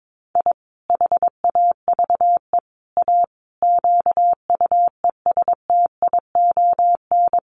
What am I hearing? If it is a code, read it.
Morse code, heard as a keyed tone, about 22 wpm, 700 Hz. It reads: I HAVE A QUESTION
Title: Sets of 5 Words Encore2 22wpm